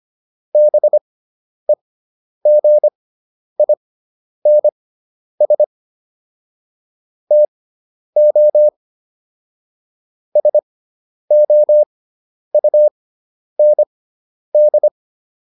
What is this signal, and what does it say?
Morse code, heard as a keyed tone, about 25 wpm, spaced as for 10 wpm, 600 Hz. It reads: BEGINS TO SOUND